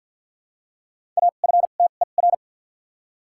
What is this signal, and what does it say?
Morse code, heard as a keyed tone, about 52 wpm, 700 Hz. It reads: A F T E R
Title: 2x Character Spacing - Top 200 Words - 50wpm